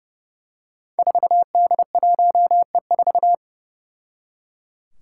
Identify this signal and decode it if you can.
Morse code, heard as a keyed tone, about 29 wpm, 700 Hz. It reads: 4D1E4
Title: Five Challenging Letter-Number Combinations 30wpm